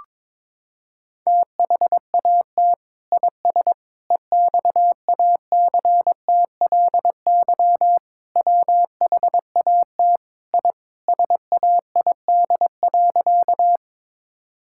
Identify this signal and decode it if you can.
Morse code, heard as a keyed tone, about 22 wpm, 700 Hz. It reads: THAT IS EXACTLY WHAT I SAID.
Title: Sentences from Top 1000 Words - Review Part 6 - 22wpm